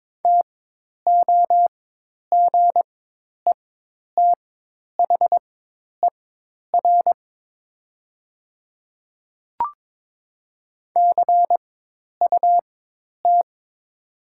Morse code, heard as a keyed tone, about 22 wpm, spaced as for 10 wpm, 700 Hz. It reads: TOGETHER CUT
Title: 4x Character Spacing - Top 300 Words - 22wpm